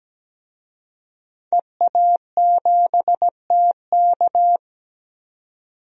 Morse code, heard as a keyed tone, about 17 wpm, 700 Hz. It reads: EA7TK
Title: DXCC-100 Call Signs - 5 Characters - 17wpm